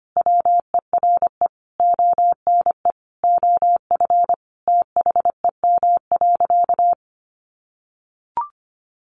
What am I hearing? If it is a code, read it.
Morse code, heard as a keyed tone, about 24 wpm, 700 Hz. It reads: WERE ONE OF THEM.
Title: Sentences from Top 1500 Words 25wpm